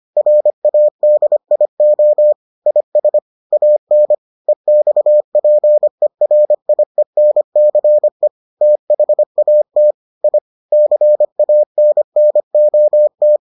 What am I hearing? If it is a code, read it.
Morse code, heard as a keyed tone, about 25 wpm, 600 Hz. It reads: RADIO IS AN EXPERIENCE THAT I CANNOT